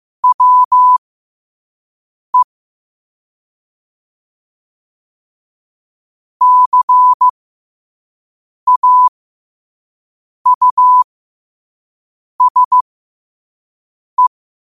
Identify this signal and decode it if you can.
Morse code, heard as a keyed tone, about 15 wpm, spaced as for 5 wpm, 1000 Hz. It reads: WE CAUSE
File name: qotdfiles / 5wpm